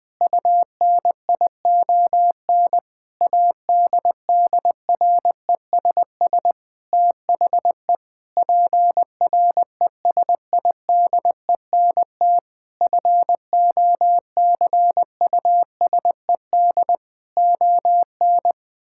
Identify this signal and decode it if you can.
Morse code, heard as a keyed tone, about 20 wpm, 700 Hz. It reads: UNION ADDRESS THE PRESIDENT FOCUSED ON
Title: Sentences with 7-Word Phrases - 20wpm